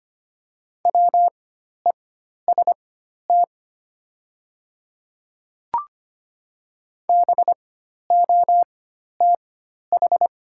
Morse code, heard as a keyed tone, about 25 wpm, spaced as for 12 wpm, 690 Hz. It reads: WEST BOTH
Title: 4x Character Spacing - Top 500 Words - 25wpm